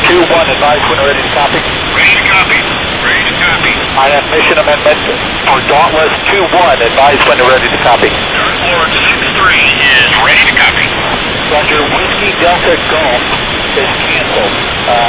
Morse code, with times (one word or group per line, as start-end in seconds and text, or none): none